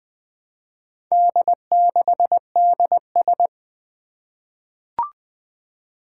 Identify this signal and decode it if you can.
Morse code, heard as a keyed tone, about 20 wpm, 700 Hz. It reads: D6DS
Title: Four Challenging Letter-Number Combinations 20wpm